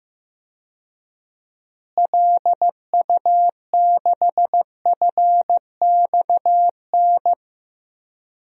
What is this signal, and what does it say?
Morse code, heard as a keyed tone, about 15 wpm, 700 Hz. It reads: LU6FXN
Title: DXCC-100 Call Signs - 6 Characters - Repeated 2x - 15wpm